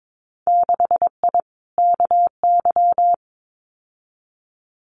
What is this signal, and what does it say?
Morse code, heard as a keyed tone, about 22 wpm, 700 Hz. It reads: 6I KY
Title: Winter Field Day Exchanges 22wpm